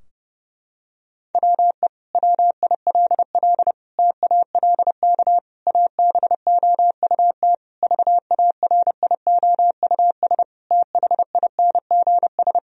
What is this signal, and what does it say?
Morse code, heard as a keyed tone, about 32 wpm, 700 Hz. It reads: WE WILL TALK ABOUT VARIOUS THINGS
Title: Sentences from Top 1000 Words - Review Part 4 - 30wpm